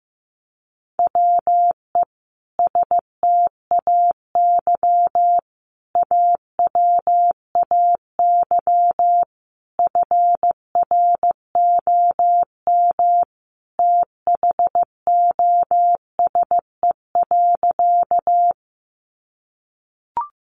Morse code, heard as a keyed tone, about 15 wpm, 700 Hz. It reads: WE STAY AWAY FROM THOSE.
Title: A Sound Of Thunder 15wpm